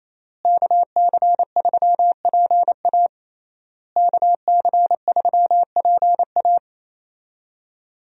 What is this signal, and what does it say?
Morse code, heard as a keyed tone, about 28 wpm, 700 Hz. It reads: KC3PA KC3PA
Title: USA Call Signs - 5 Characters - Repeated 3x - 28wpm